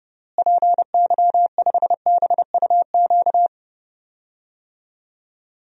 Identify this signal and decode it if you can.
Morse code, heard as a keyed tone, about 32 wpm, 700 Hz. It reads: PY5BUQ
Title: DXCC-100 Call Signs - 6 Characters - Repeated 3x - 30wpm